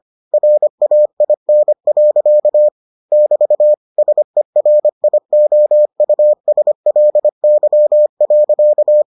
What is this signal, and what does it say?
Morse code, heard as a keyed tone, about 25 wpm, 600 Hz. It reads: RAIN. <BT> SERIOUSLY.